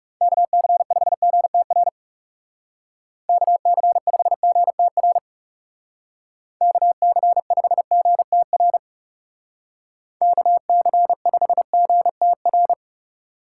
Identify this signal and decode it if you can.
Morse code, heard as a keyed tone, about 40 wpm, 700 Hz. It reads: KC5GTR KC5GTR KC5GTR KC5GTR